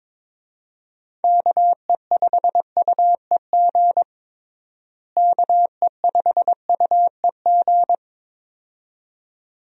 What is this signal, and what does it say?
Morse code, heard as a keyed tone, about 22 wpm, 700 Hz. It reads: KE5UEG KE5UEG
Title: USA Call Signs - 6 Characters - Repeated 2x - 22wpm